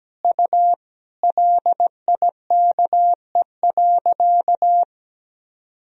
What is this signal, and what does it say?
Morse code, heard as a keyed tone, about 17 wpm, 700 Hz. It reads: U LIKE.
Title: Sentences from Top 500 Words and Abbreviations 17wpm